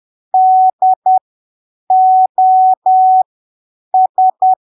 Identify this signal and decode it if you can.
Morse code, heard as a keyed tone, about 10 wpm, spaced as for 7 wpm, 750 Hz. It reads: DOS
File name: arquivos / texto / texto 08 10ppm